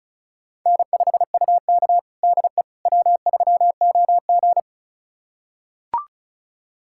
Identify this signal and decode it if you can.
Morse code, heard as a keyed tone, about 36 wpm, 700 Hz. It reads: N5UK DE W3OG
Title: USA Callsign DE Callsign - 4 Characters - 35wpm